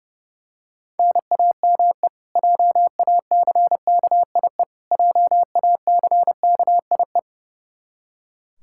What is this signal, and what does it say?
Morse code, heard as a keyed tone, about 30 wpm, 700 Hz. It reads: NAME JACKIE JACKIE
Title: Practice QSO - Exchange 1 - Beginning - 30wpm